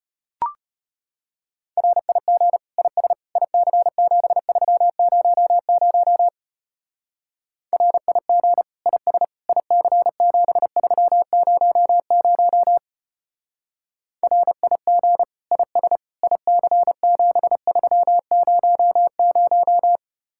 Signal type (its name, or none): Morse code